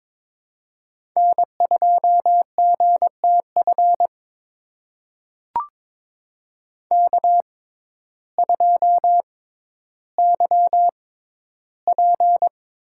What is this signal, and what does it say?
Morse code, heard as a keyed tone, about 22 wpm, spaced as for 12 wpm, 700 Hz. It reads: N2GTF K2YP
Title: USA Call Signs - 6x Character Spacing - 22wpm